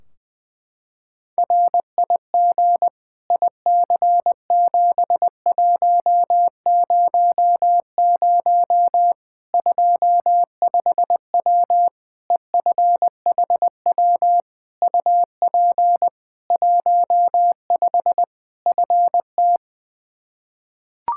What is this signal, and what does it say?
Morse code, heard as a keyed tone, about 20 wpm, 700 Hz. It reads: RIG IC7100 25W EFHW UP 15 FT
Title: Practice QSO - Exchange 2 - Info - 20wpm